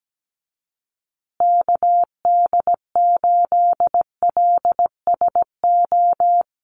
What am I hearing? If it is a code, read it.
Morse code, heard as a keyed tone, about 17 wpm, 700 Hz. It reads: KD8LSO
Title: USA Call Signs Repeated 2x 17wpm